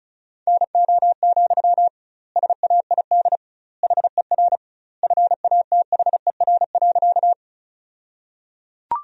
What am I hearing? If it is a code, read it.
Morse code, heard as a keyed tone, about 34 wpm, 700 Hz. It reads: NO, SAID HER FATHER.
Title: 2x Word-Spacing - Sentences from Top 300 Words - 35wpm